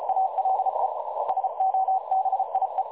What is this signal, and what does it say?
Morse code, heard as a keyed tone, about 34 wpm, 760 Hz, with noise. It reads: ALE SOBIE
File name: dlazaawansowanych / nagraniazpasma / websdr recording 3565.0kHz